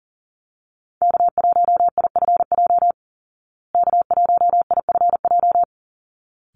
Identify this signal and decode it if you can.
Morse code, heard as a keyed tone, about 40 wpm, 700 Hz. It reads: K1IFJ K1IFJ
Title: Call Signs Repeated 3x - 40wpm